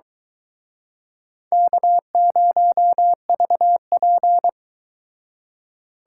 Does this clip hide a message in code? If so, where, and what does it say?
Yes: Morse code from 1.5 to 4.5 s, heard as a keyed tone, about 24 wpm, 700 Hz. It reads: K0VP